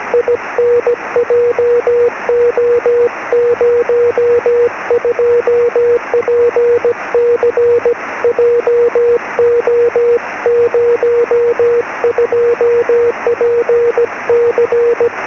Morse code, heard as a keyed tone, about 17 wpm, 480 Hz, with noise. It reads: IN JO02PC JO02PC
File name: sm7lcb / g4hup 23cm